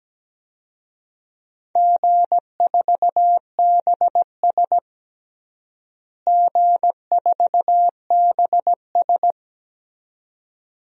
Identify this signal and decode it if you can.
Morse code, heard as a keyed tone, about 17 wpm, 700 Hz. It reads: G4BS G4BS
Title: DXCC-100 Call Signs - 4 Characters - Repeated 2x - 17wpm